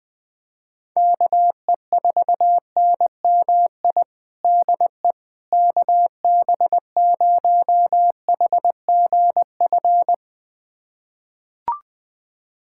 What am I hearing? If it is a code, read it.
Morse code, heard as a keyed tone, about 20 wpm, 700 Hz. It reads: KE4NMI DE KB0HGF
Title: USA Callsign DE Callsign - 6 Characters - 20wpm